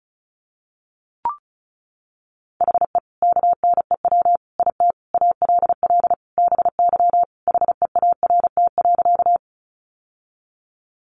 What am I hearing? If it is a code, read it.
Morse code, heard as a keyed tone, about 34 wpm, 700 Hz. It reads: HE KNEW IT ALL BY HEART.
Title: Sentences from Top 1200 Words 35wpm